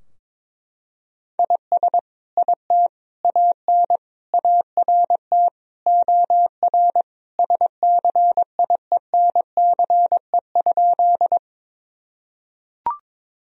Morse code, heard as a keyed tone, about 22 wpm, 700 Hz. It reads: IS IT AN ART OR SCIENCE?
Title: Sentences from Top 1000 Words - Review Part 3 - 22wpm